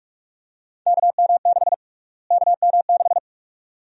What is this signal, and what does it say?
Morse code, heard as a keyed tone, about 41 wpm, 700 Hz. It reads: KM6 KM6
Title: USA Call Sign Prefixes - Repeated 2x - 45wpm